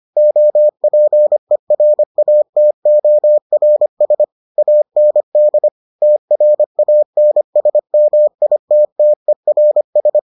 Morse code, heard as a keyed tone, about 25 wpm, 600 Hz. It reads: OPERATORS AND TRANSMITTERS